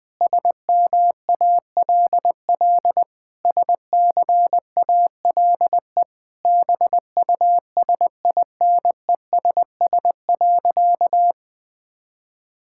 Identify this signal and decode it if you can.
Morse code, heard as a keyed tone, about 21 wpm, 700 Hz. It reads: SMALL SCALE BUSINESS.
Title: Sentences from Top 1000 Words - Review Part 5 - 20wpm